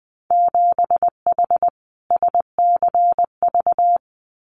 Morse code, heard as a keyed tone, about 20 wpm, 700 Hz. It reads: 7H SCV
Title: Winter Field Day Exchanges Encore 20wpm